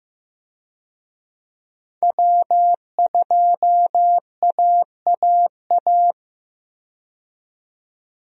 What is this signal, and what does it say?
Morse code, heard as a keyed tone, about 15 wpm, 700 Hz. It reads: W2AAA
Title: USA Call Signs - 5 Characters - 15wpm